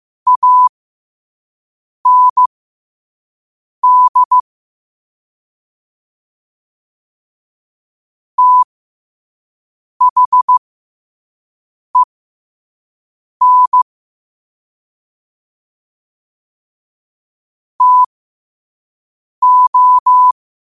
Morse code, heard as a keyed tone, about 15 wpm, spaced as for 5 wpm, 1000 Hz. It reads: AND THEN TO
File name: qotdfiles / 5wpm